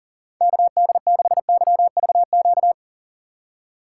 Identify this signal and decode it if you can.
Morse code, heard as a keyed tone, about 39 wpm, 700 Hz. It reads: KD6YVQ
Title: USA Call Signs - 6 Characters - 6x Character Spacing - 40wpm